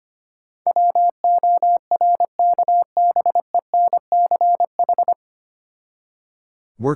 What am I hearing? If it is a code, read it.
Morse code, heard as a keyed tone, about 25 wpm, 700 Hz. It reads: WORKBENCH
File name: practice / 25wpm / Difficult Words with Hints - 25wpm